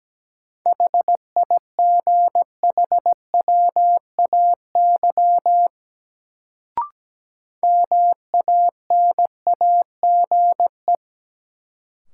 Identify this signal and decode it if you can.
Morse code, heard as a keyed tone, about 17 wpm, 700 Hz. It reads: HIGHWAY MANAGE
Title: Sentences from Top 1400 Words - New Word Drill - 17wpm